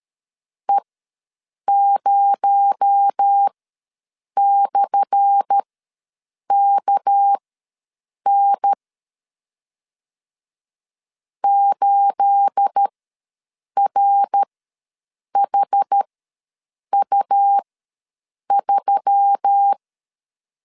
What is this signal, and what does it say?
Morse code, heard as a keyed tone, about 13 wpm, spaced as for 7 wpm, 800 Hz. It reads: E0/KN 8RHU3
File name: LEZIONI / 10 SIGNS / 10A SLASH ws 12wpm 3443